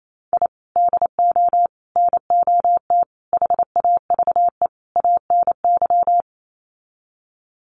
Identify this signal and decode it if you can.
Morse code, heard as a keyed tone, about 28 wpm, 700 Hz. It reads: I DO NOT HAVE ANY
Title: Sets of 5 Words 28wpm